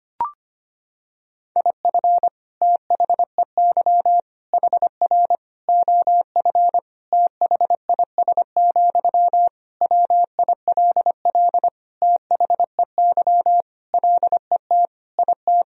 Morse code, heard as a keyed tone, about 25 wpm, 700 Hz. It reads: IF THEY HR OF THIS, WILL THEY LET IT
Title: Sentences from Top 400 Words and Abbreviations 25wpm